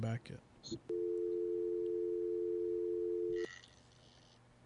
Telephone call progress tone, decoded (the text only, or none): dial tone